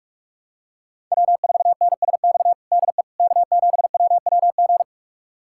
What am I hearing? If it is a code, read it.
Morse code, heard as a keyed tone, about 46 wpm, 700 Hz. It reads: W4NSX DE K7WWG
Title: USA Callsign DE Callsign 45wpm